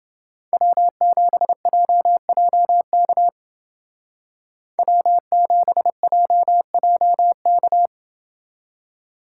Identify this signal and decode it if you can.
Morse code, heard as a keyed tone, about 29 wpm, 700 Hz. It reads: W7JJK W7JJK